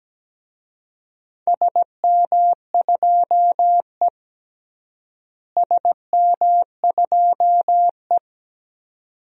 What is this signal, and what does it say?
Morse code, heard as a keyed tone, about 17 wpm, 700 Hz. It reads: SM2E SM2E
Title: DXCC-100 Call Signs - 4 Characters - Repeated 2x - 17wpm